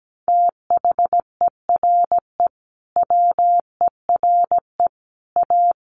Morse code, heard as a keyed tone, about 17 wpm, 700 Hz. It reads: THERE WERE A
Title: Sets of 5 Words Encore2 17wpm